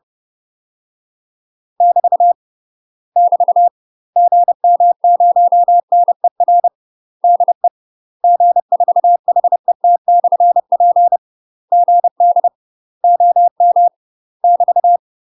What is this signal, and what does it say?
Morse code, heard as a keyed tone, about 30 wpm, 700 Hz. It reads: <BT> <BT> GM0NER DE G4HET/P GD OM <BT>
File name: UK MASTER QSOs TRAINING / 30WPM / UK MASTER QSOs 0043